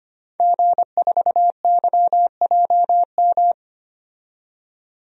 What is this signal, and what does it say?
Morse code, heard as a keyed tone, about 25 wpm, 700 Hz. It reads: G4YJM
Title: DXCC-100 Call Signs - 5 Characters - 25wpm